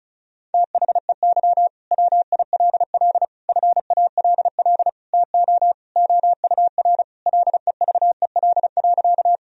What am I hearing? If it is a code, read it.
Morse code, heard as a keyed tone, about 34 wpm, 700 Hz. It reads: THEY WILL FALL TO OUR LEVEL.
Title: Sentences from Top 1000 Words - Review Part 4 - 35wpm